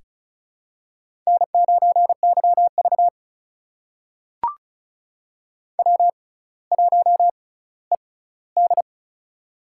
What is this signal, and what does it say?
Morse code, heard as a keyed tone, about 36 wpm, spaced as for 18 wpm, 700 Hz. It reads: N9YV W1ED
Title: USA Call Signs - 4 Characters - 6x Character Spacing - 35wpm